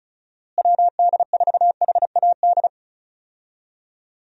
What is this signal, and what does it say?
Morse code, heard as a keyed tone, about 36 wpm, 700 Hz. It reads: WD4HAD
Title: USA Call Signs - 6 Characters - Repeated 3x - 35wpm